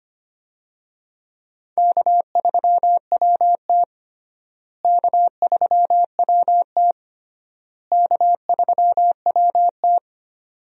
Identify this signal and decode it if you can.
Morse code, heard as a keyed tone, about 25 wpm, 700 Hz. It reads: K3WT K3WT K3WT
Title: USA Call Signs - 4 Characters - Repeated 3x - 25wpm